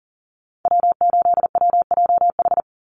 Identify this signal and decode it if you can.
Morse code, heard as a keyed tone, about 40 wpm, 700 Hz. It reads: W8WJH
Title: USA Call Signs 40wpm